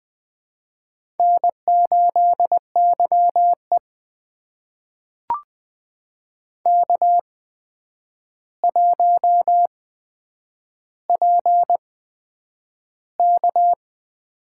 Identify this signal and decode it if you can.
Morse code, heard as a keyed tone, about 20 wpm, spaced as for 9 wpm, 700 Hz. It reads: N8YE K1PK
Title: USA Call Signs - 4 Characters - 8x Character Spacing - 20wpm